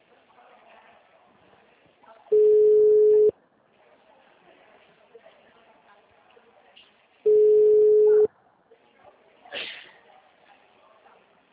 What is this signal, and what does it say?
Ringback tone (Europe, 425 Hz), with speech.